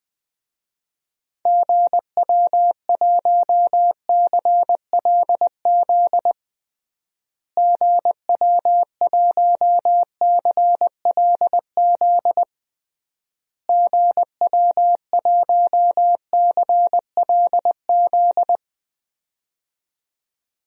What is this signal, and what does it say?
Morse code, heard as a keyed tone, about 20 wpm, 700 Hz. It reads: GW1CLZ GW1CLZ GW1CLZ
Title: DXCC-100 Call Signs - 6 Characters - Repeated 3x - 20wpm